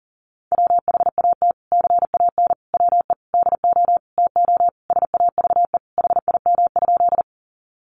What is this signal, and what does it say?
Morse code, heard as a keyed tone, about 40 wpm, 700 Hz. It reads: WHAT CAN WE DO TO SAVE HIM?
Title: Sentences from Top 1100 Words 40wpm